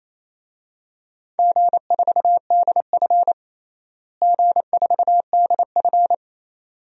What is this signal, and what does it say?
Morse code, heard as a keyed tone, about 28 wpm, 700 Hz. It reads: G4DF G4DF
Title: DXCC-100 Call Signs - 4 Characters - Repeated 2x - 28wpm